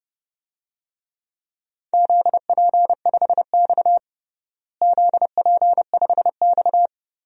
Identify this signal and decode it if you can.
Morse code, heard as a keyed tone, about 30 wpm, 700 Hz. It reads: ZP5X ZP5X
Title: DXCC-100 Call Signs - 4 Characters - Repeated 2x - 30wpm